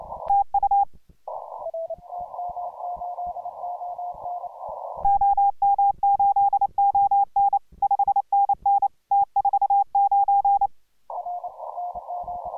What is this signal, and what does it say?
Morse code, heard as a keyed tone, about 29 wpm, 810 Hz. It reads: TU OM8ON 5NN T49